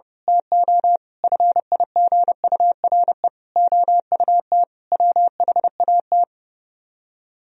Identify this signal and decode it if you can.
Morse code, heard as a keyed tone, about 28 wpm, 700 Hz. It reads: TO FIGURE OUT WHAT
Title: Speed-Racing - Sets of 5 Words 30wpm